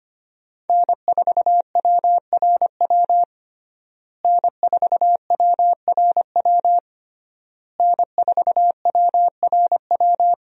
Morse code, heard as a keyed tone, about 25 wpm, 700 Hz. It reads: N4WRW N4WRW N4WRW
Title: USA Call Signs - 5 Characters - Repeated 3x - 25wpm